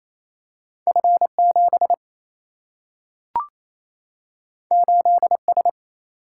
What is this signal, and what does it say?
Morse code, heard as a keyed tone, about 28 wpm, 700 Hz. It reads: F 7 8 S
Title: Single-Letter Single-Number 28wpm